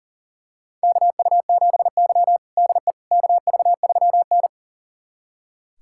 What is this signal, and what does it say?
Morse code, heard as a keyed tone, about 40 wpm, 700 Hz. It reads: KU7Y DE KV3N
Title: USA Callsign DE Callsign - 4 Characters - 40wpm